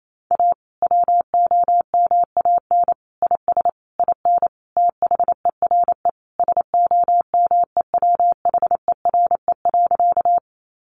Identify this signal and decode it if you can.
Morse code, heard as a keyed tone, about 28 wpm, 700 Hz. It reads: A WOMAN IS IN THERE SOMEWHERE.
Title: Sentences from Top 1400 Words 28wpm